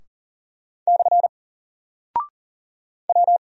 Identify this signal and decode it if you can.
Morse code, heard as a keyed tone, about 39 wpm, 700 Hz. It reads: / W
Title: Single Character (Letter:Number:Punctuation:Prosign) 40wpm